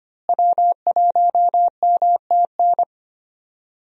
Morse code, heard as a keyed tone, about 25 wpm, 700 Hz. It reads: W1MTN
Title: USA Call Signs - 5 Characters - Repeated 3x - 25wpm